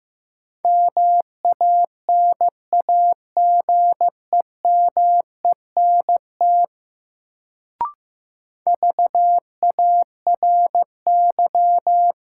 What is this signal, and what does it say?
Morse code, heard as a keyed tone, about 15 wpm, 700 Hz. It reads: MANAGEMENT VARY
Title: Sentences from Top 1200 Words - New Word Drill Alt - 15wpm